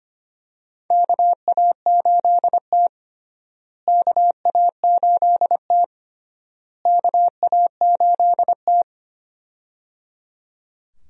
Morse code, heard as a keyed tone, about 25 wpm, 700 Hz. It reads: KA8T KA8T KA8T
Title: USA Call Signs - 4 Characters - Repeated 3x - 25wpm